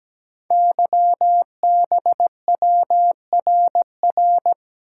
Morse code, heard as a keyed tone, about 17 wpm, 700 Hz. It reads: YBWRR
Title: Code Groups - 5 Letters - 17wpm